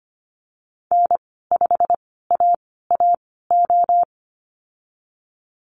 Morse code, heard as a keyed tone, about 25 wpm, spaced as for 16 wpm, 690 Hz. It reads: N5AAO
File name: Lesson 15g - Number 5 - Call Signs 25wpm